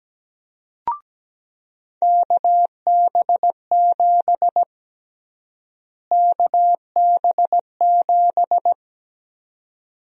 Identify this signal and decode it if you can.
Morse code, heard as a keyed tone, about 17 wpm, 700 Hz. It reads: KB7 KB7
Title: USA Call Sign Prefixes - Repeated 2x - 17wpm